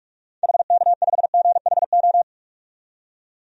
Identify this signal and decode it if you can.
Morse code, heard as a keyed tone, about 47 wpm, 700 Hz. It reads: HK5GHO
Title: DXCC-100 Call Signs - 6 Characters - 45wpm